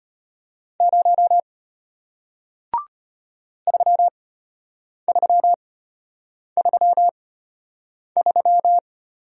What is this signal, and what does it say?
Morse code, heard as a keyed tone, about 34 wpm, spaced as for 8 wpm, 700 Hz. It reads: 0 3333